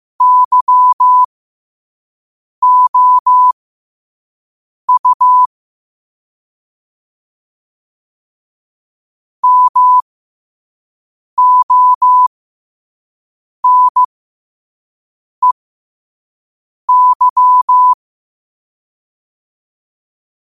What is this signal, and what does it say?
Morse code, heard as a keyed tone, about 15 wpm, spaced as for 5 wpm, 1000 Hz. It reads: YOU MONEY
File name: qotdfiles / 5wpm